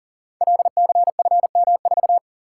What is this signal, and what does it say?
Morse code, heard as a keyed tone, about 43 wpm, 700 Hz. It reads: LCFM4
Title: Code Groups - 5 Letter:Numbers - 40wpm